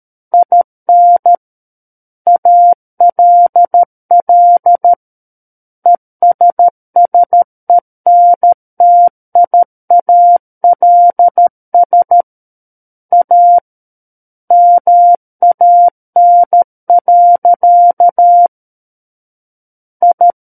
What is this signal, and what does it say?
Morse code, heard as a keyed tone, about 13 wpm, 700 Hz. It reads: IN ALL ESSENTIALS A MAN. I
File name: Book One / 04-Chapter 4 - 13 WPM